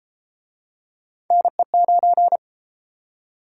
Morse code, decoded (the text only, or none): NE9